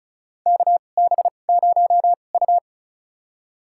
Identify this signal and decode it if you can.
Morse code, heard as a keyed tone, about 36 wpm, spaced as for 25 wpm, 700 Hz. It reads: KB0U
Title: USA Call Signs - 2x Character Spacing - 35wpm